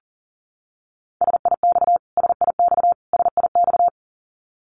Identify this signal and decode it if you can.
Morse code, heard as a keyed tone, about 40 wpm, 700 Hz. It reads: SIX SIX SIX
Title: Top 500 Words Repeated 40wpm